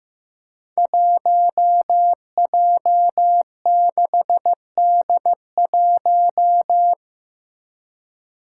Morse code, heard as a keyed tone, about 15 wpm, 700 Hz. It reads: 1J6D1
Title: Five Challenging Letter-Number Combinations 15wpm